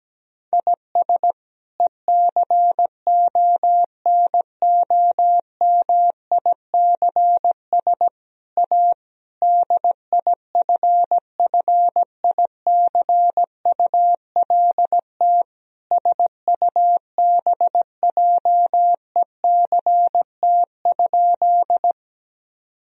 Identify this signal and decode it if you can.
Morse code, heard as a keyed tone, about 17 wpm, 700 Hz. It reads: IS ECONOMICS A DIFFICULT SUBJECT?
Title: Sentences from Top 1000 Words - Review Part 3 - 17wpm